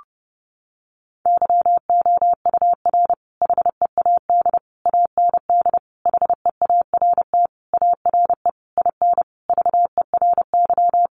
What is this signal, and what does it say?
Morse code, heard as a keyed tone, about 30 wpm, 700 Hz. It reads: YOUR HEAD AND HEART ARE IN VERY
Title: Sentences from Top 1300 Words 30wpm